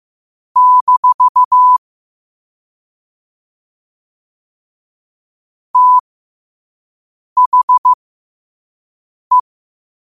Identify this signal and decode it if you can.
Morse code, heard as a keyed tone, about 15 wpm, spaced as for 5 wpm, 1000 Hz. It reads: - THE
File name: qotdfiles / 5wpm